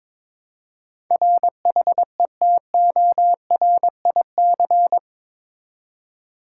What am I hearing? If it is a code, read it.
Morse code, heard as a keyed tone, about 22 wpm, 700 Hz. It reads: RHETORIC
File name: practice / 22wpm / Eight Letter Words - 22wpm